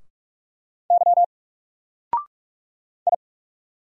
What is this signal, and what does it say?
Morse code, heard as a keyed tone, about 44 wpm, 700 Hz. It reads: Y I
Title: Single Letter-Number 45wpm